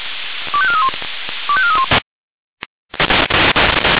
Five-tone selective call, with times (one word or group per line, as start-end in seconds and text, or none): 0.5-0.9 25431
1.5-1.9 25431